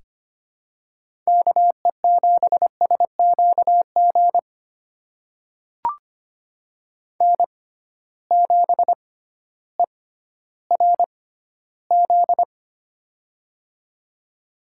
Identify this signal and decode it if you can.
Morse code, heard as a keyed tone, about 25 wpm, spaced as for 14 wpm, 700 Hz. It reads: KE7SQG N7ERZ
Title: USA Call Signs - 6x Character Spacing - 25wpm